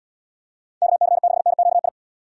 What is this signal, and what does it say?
Morse code, heard as a keyed tone, about 74 wpm, 700 Hz. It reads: KC6NQI